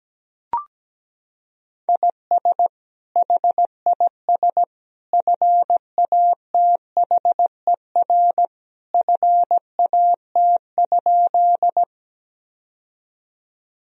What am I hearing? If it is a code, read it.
Morse code, heard as a keyed tone, about 17 wpm, 700 Hz. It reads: IS HIS FATHER FAT?
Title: Sentences from Top 1000 Words - Review Part 4 - 17wpm